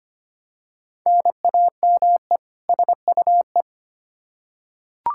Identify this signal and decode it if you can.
Morse code, heard as a keyed tone, about 25 wpm, 700 Hz. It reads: NAME SUE
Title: Practice QSO - Exchange 1 - Beginning - 25wpm